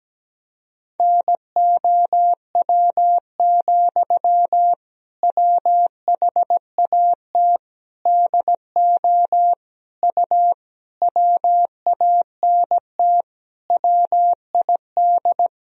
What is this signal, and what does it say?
Morse code, heard as a keyed tone, about 17 wpm, 700 Hz. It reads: NOW, WHAT DO U WANT WID
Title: Sentences from Top 200 Words and Abbreviations 17wpm